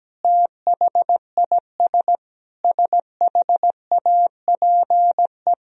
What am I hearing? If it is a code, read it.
Morse code, heard as a keyed tone, about 17 wpm, 700 Hz. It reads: THIS SHAPE
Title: Sentences from Top 1000 Words - Review Part 3 - 17wpm